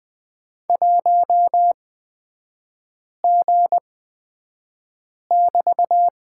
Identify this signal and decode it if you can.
Morse code, heard as a keyed tone, about 20 wpm, spaced as for 5 wpm, 700 Hz. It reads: 1G<BT>
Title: 224 Lesson 36e - BT - Three Characters 20wpm